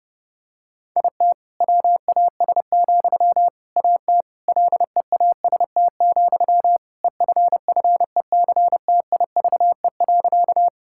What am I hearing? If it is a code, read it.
Morse code, heard as a keyed tone, about 30 wpm, 700 Hz. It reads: IT WAS, AT LEAST, EFFECTIVE.
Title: Sentences from Top 1000 Words - Review Part 6 - 30wpm